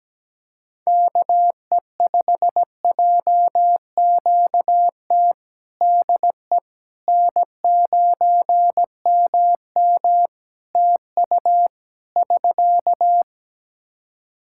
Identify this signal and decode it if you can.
Morse code, heard as a keyed tone, about 17 wpm, 700 Hz. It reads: KE5JQT DE N9MM TU <SK>
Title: K1USN SST QSO Practice 17wpm